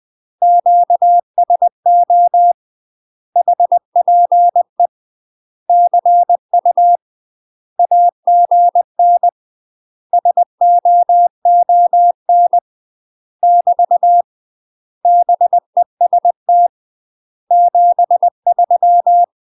Morse code, heard as a keyed tone, about 20 wpm, 700 Hz. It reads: QSO HPE CU AGN SOON <BT> BEST 73